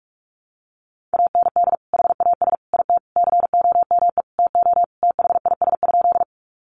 Morse code, heard as a keyed tone, about 44 wpm, 700 Hz. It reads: AND HAS IT COME TO THIS?
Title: Sentences from Top 100 Words 45wpm